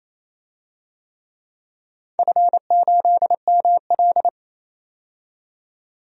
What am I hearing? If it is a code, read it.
Morse code, heard as a keyed tone, about 28 wpm, 700 Hz. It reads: F8ML
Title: DXCC-100 Call Signs - 4 Characters - 28wpm